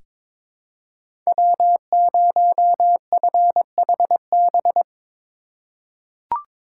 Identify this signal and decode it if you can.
Morse code, heard as a keyed tone, about 22 wpm, 700 Hz. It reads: W0FHB
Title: USA Call Signs - 5 Characters - 2x Character Spacing - 22wpm